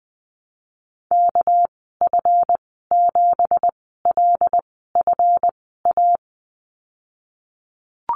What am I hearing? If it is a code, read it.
Morse code, heard as a keyed tone, about 20 wpm, spaced as for 15 wpm, 700 Hz. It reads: KF7LFA
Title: 231 Lesson 37g - Letter x - Call Signs 20wpm